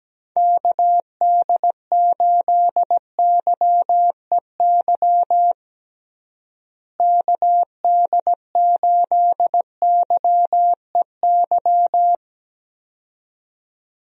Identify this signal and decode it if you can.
Morse code, heard as a keyed tone, about 17 wpm, 700 Hz. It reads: KD8YEY KD8YEY
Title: USA Call Signs - 6 Characters - Repeated 2x - 17wpm